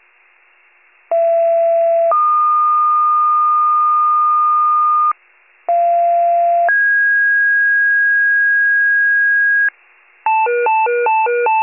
Two-tone page, heard as a one-second tone, then a three-second tone, about 670 then 1220 Hz, about 690 then 1650 Hz, with noise. It is heard from 1.1 to 9.7 s.